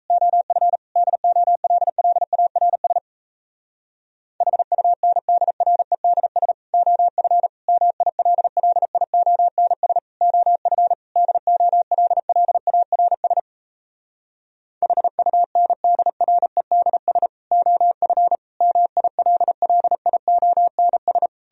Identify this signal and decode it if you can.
Morse code, heard as a keyed tone, about 38 wpm, 700 Hz. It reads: OF DOLLARS HUNDREDS OF MILLIONS OF DOLLARS HUNDREDS OF MILLIONS